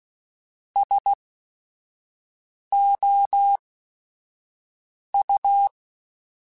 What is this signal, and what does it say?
Morse code, heard as a keyed tone, about 16 wpm, spaced as for 5 wpm, 800 Hz. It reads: SOU
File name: Technical Subjects / Technical Subjects-mp3s / XC @5